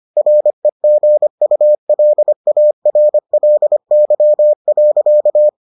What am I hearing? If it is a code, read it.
Morse code, heard as a keyed tone, about 25 wpm, 600 Hz. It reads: REGULARLY.